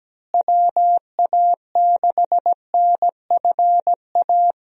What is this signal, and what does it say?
Morse code, heard as a keyed tone, about 17 wpm, 700 Hz. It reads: WA6NFA
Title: USA Call Signs - 6 Characters - Repeated 2x - 17wpm